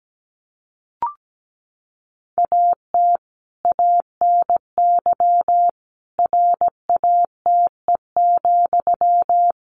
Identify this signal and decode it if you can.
Morse code, heard as a keyed tone, about 17 wpm, 700 Hz. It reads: AT ANY RATE,
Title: Sentences from Top 1300 Words 17wpm